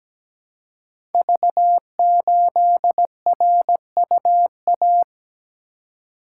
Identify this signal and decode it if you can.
Morse code, heard as a keyed tone, about 17 wpm, 700 Hz. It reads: V8RUA
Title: Code Groups - 5 Letter:Numbers - 17wpm